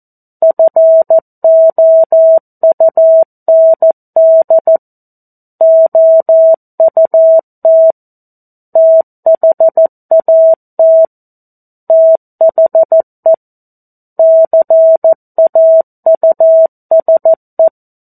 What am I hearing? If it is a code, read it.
Morse code, heard as a keyed tone, about 14 wpm, 650 Hz. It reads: FOUND OUT THAT THE CAUSE
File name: 02-Chapter-02 14 WPM